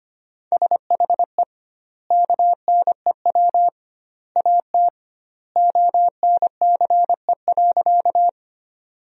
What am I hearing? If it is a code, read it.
Morse code, heard as a keyed tone, about 25 wpm, 700 Hz. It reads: SHE KNEW AT ONCE.
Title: 2x Word-Spacing - Sentences from Top 400 Words - 25wpm